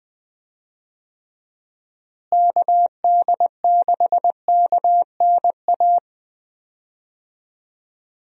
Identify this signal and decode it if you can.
Morse code, heard as a keyed tone, about 20 wpm, 700 Hz. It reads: KD6KNA
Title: USA Call Signs - 6 Characters - Repeated 2x - 20wpm